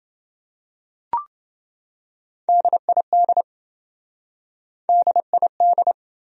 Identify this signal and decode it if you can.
Morse code, heard as a keyed tone, about 29 wpm, 700 Hz. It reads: DID DID